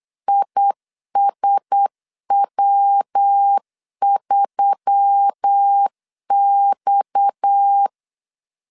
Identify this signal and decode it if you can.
Morse code, heard as a keyed tone, about 8 wpm, 800 Hz. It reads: ISW3X
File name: CW / DET / 1 9 1500C S B 8wpm 5624